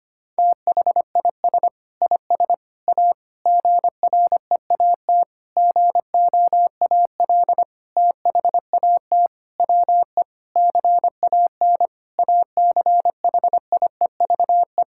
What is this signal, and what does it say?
Morse code, heard as a keyed tone, about 25 wpm, 700 Hz. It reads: THIS IS A GREAT GOAL THAT WE CAN ACHIEVE
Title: Sentences from Top 1000 Words - Review Part 6 - 25wpm